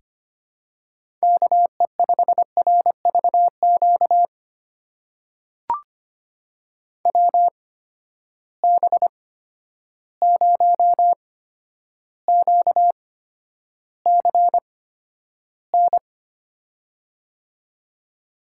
Morse code, heard as a keyed tone, about 25 wpm, spaced as for 11 wpm, 700 Hz. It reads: KE5RVQ WB0QCN
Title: USA Call Signs - 6 Characters - 8x Character Spacing - 25wpm